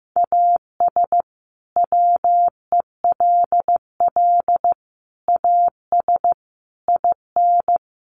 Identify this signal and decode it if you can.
Morse code, heard as a keyed tone, about 15 wpm, 700 Hz. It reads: AS WELL AS IN
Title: Sets of 5 Words Encore 15wpm